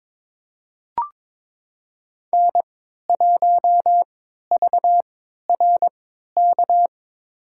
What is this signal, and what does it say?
Morse code, heard as a keyed tone, about 22 wpm, spaced as for 13 wpm, 700 Hz. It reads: N1VRK
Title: USA Call Signs - 3x Character Spacing - 22wpm